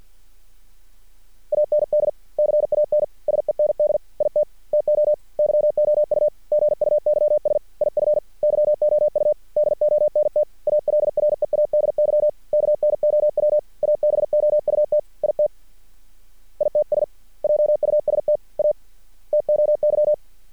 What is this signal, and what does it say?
Morse code, heard as a keyed tone, about 50 wpm, 600 Hz. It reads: AND CAN SEND IT TO YOU GUYS IF YOU DONT ALREADY KNOW ABOUT IT ITS JUST A TOY
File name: archived / CRSnet